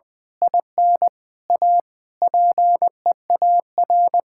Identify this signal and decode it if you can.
Morse code, heard as a keyed tone, about 20 wpm, 700 Hz. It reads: IN A PEAR
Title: Sentences with 9-Word Phrases - 20wpm